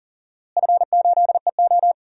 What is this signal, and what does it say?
Morse code, heard as a keyed tone, about 40 wpm, 700 Hz. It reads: F8EO
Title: DXCC-100 Call Signs - 4 Characters - Repeated 3x - 40wpm